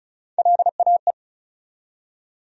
Morse code, heard as a keyed tone, about 36 wpm, 700 Hz. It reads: LAE
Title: USA Call Sign Postfixes - 3 Letters 35wpm